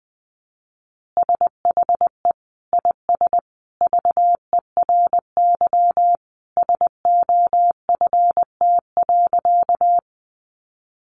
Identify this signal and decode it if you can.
Morse code, heard as a keyed tone, about 20 wpm, 700 Hz. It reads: SHE IS VERY SOFT.
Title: Easy Phrases 20wpm